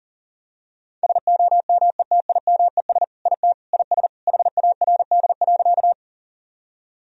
Morse code, heard as a keyed tone, about 40 wpm, 700 Hz. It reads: SOMETIMES IT IS HARD.
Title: Sentences from Top 1000 Words - Review Part 3 - 40wpm